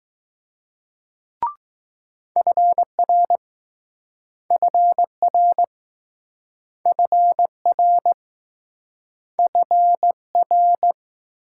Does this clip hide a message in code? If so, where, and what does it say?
Yes: Morse code from 2.4 to 10.9 s, heard as a keyed tone, about 20 wpm, 700 Hz. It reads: FR FR FR F R